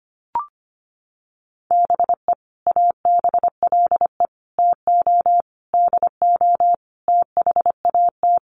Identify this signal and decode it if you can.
Morse code, heard as a keyed tone, about 25 wpm, 700 Hz. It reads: BE ABLE TO DO THAT
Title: Sets of 5 Words Encore2 25wpm